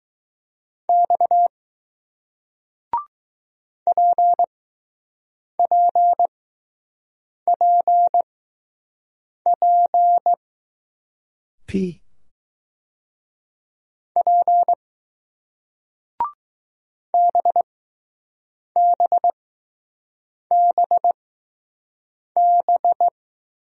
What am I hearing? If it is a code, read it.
Morse code, heard as a keyed tone, about 20 wpm, spaced as for 6 wpm, 700 Hz. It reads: X PPPP P BBBB